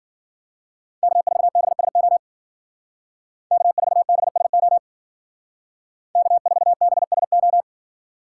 Morse code, heard as a keyed tone, about 49 wpm, 700 Hz. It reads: K4BSO K4BSO K4BSO
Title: Speed Racing - USA 1x3 Call Signs 40wpm